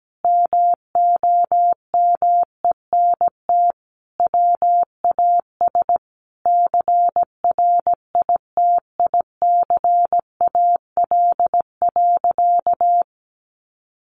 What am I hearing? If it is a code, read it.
Morse code, heard as a keyed tone, about 17 wpm, 700 Hz. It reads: MOMENT WAS CRITICAL.
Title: Sentences from Top 1100 Words 17wpm